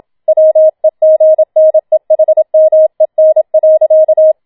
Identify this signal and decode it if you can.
Morse code, heard as a keyed tone, about 27 wpm, 630 Hz. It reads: WEGNEHMEN.